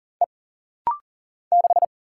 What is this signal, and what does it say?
Morse code, heard as a keyed tone, about 40 wpm, 700 Hz. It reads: E 6
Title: Single Letter-Number - Rapid-Fire 40wpm